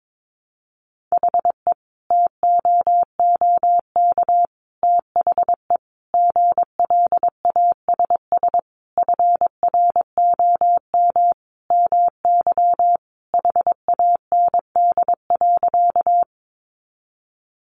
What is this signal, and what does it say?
Morse code, heard as a keyed tone, about 22 wpm, 700 Hz. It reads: HE TOOK THE GLASS FROM MY HAND.
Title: Sentences from Top 1300 Words 22wpm